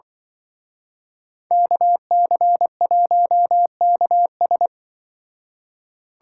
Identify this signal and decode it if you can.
Morse code, heard as a keyed tone, about 24 wpm, 700 Hz. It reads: KC1KS
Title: Speed Racing - USA Call Signs 20wpm